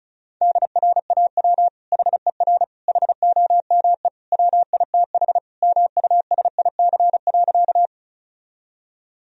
Morse code, heard as a keyed tone, about 36 wpm, 700 Hz. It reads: DRAW HER HOME WITH MUSIC.
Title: Sentences from Top 1000 Words - Review Part 1 - 35wpm